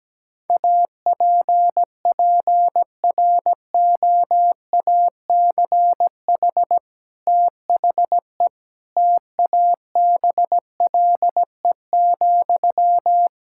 Morse code, heard as a keyed tone, about 17 wpm, 700 Hz. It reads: APPROACH THE TABLE,
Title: Sentences from Top 1000 Words - Review Part 6 - 17wpm